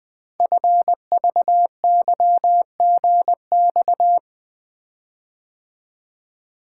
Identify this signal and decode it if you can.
Morse code, heard as a keyed tone, about 21 wpm, 700 Hz. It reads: FVYGX
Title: Code Groups - 5 Letters - 20wpm